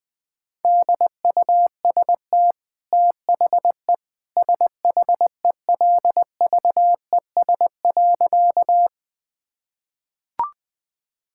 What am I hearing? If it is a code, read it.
Morse code, heard as a keyed tone, about 19 wpm, 700 Hz. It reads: DUST THE SHELVES.
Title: Easy Sentences with Hints - 20wpm